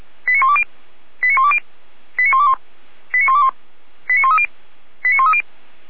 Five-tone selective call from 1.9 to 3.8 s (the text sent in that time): 89121 89121